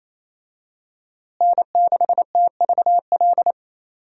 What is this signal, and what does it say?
Morse code, heard as a keyed tone, about 28 wpm, 700 Hz. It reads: N6TVL
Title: USA Call Signs - 5 Characters - Repeated 2x - 28wpm